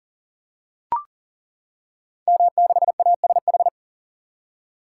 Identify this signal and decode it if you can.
Morse code, heard as a keyed tone, about 41 wpm, 700 Hz. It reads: M6ASH
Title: Code Groups - 5 Letter:Numbers - 40wpm